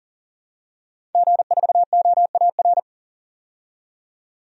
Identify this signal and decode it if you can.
Morse code, heard as a keyed tone, about 40 wpm, 700 Hz. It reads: G4OAR
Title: DXCC-100 Call Signs - 5 Characters - 40wpm